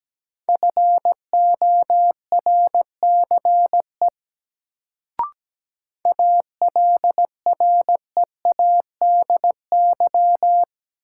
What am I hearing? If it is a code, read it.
Morse code, heard as a keyed tone, about 17 wpm, 700 Hz. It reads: FORCE ALREADY
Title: Sentences from Top 700 Words - New Word Drill - 17wpm